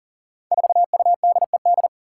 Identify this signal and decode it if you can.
Morse code, heard as a keyed tone, about 43 wpm, 700 Hz. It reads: 4UDED
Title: Five Challenging Letter-Number Combinations 40wpm